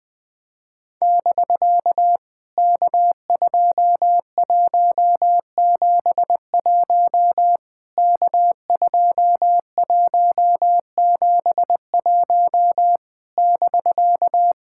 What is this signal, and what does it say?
Morse code, heard as a keyed tone, about 20 wpm, 700 Hz. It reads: <BK> K2171 K2171 <BK>
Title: POTA QSO Practice - 20wpm